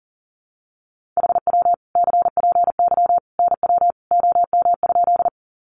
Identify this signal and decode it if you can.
Morse code, heard as a keyed tone, about 40 wpm, 700 Hz. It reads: HW CPY NW OM?
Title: Common QSO Phrases 40wpm